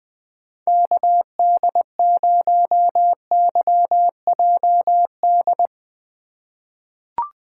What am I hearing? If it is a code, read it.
Morse code, heard as a keyed tone, about 21 wpm, 700 Hz. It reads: KD0YJD
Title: USA Call Signs - 6 Characters - 2x Character Spacing - 20wpm